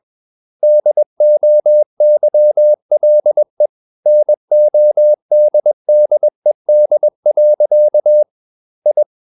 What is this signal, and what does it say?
Morse code, heard as a keyed tone, about 21 wpm, 600 Hz. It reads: DOYLE NODDED. I
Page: CW med Gnister